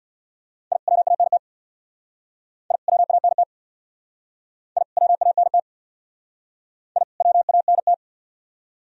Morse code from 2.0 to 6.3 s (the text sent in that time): I WANT I WANT